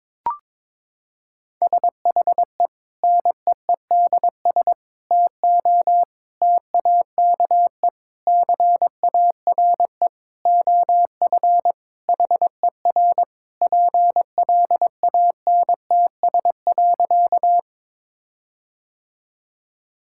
Morse code, heard as a keyed tone, about 22 wpm, 700 Hz. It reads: SHE NEEDS TO TAKE CARE OF HER PLANTS.
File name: practice / 22wpm / Sentences with 4-Word Phrases - 22wpm